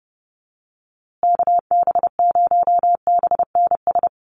Morse code, heard as a keyed tone, about 30 wpm, 700 Hz. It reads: KB0BNS
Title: USA Call Signs Repeated 2x 30wpm